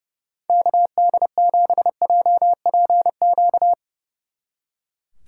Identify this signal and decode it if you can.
Morse code, heard as a keyed tone, about 36 wpm, 700 Hz. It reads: KD7JPQ
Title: Speed Racing - USA 2x3 Call Signs 30wpm